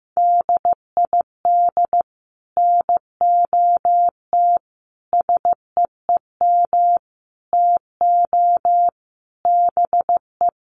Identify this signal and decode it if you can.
Morse code, heard as a keyed tone, about 15 wpm, 700 Hz. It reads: DID NOT SEEM TO BE
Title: Sets of 5 Words Encore 15wpm